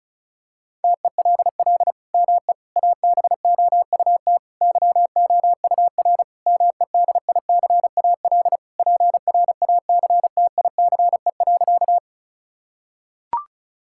Morse code, heard as a keyed tone, about 34 wpm, 700 Hz. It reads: TELL ME ABOUT YOUR MEDICAL PRACTICE.
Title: Sentences from Top 1000 Words - Review Part 5 - 35wpm